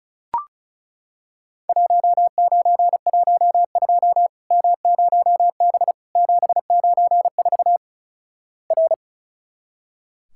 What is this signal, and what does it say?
Morse code, heard as a keyed tone, about 36 wpm, 700 Hz. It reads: 1912 M0B 794 R
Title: Worked All Europe DX Contest - QTC Exchange - 35wpm